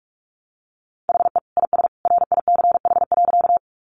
Morse code, heard as a keyed tone, about 44 wpm, 700 Hz. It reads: HE IS RICH.
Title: Sentences from Top 1100 Words 45wpm